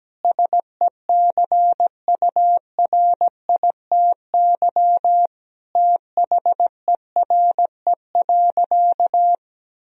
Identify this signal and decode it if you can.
Morse code, heard as a keyed tone, about 17 wpm, 700 Hz. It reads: SECURITY THERE.
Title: Sentences from Top 1000 Words - Review Part 4 - 17wpm